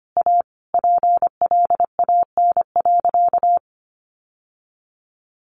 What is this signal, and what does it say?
Morse code, heard as a keyed tone, about 25 wpm, 700 Hz. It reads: A PLAN.
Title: Sentences from Top 1200 Words 25wpm